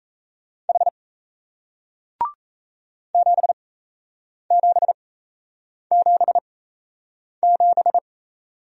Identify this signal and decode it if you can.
Morse code, heard as a keyed tone, about 38 wpm, spaced as for 8 wpm, 700 Hz. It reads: H 7777